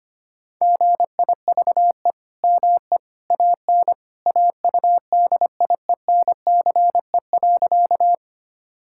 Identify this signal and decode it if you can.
Morse code, heard as a keyed tone, about 25 wpm, 700 Hz. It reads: GIVE ME AN AUDIENCE.
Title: Sentences from Top 1000 Words - Review Part 5 - 25wpm